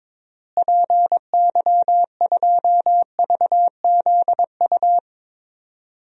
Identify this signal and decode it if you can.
Morse code, heard as a keyed tone, about 22 wpm, 700 Hz. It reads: PY2VZU
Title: DXCC-100 Call Signs - 6 Characters - Repeated 3x - 22wpm